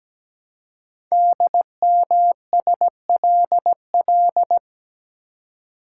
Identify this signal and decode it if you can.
Morse code, heard as a keyed tone, about 17 wpm, 700 Hz. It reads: DMSLL
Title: Code Groups - 5 Letter:Numbers - 17wpm